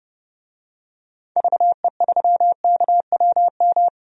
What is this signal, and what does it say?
Morse code, heard as a keyed tone, about 30 wpm, 700 Hz. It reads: VE3KWM
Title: DXCC-100 Call Signs - 6 Characters - Repeated 3x - 30wpm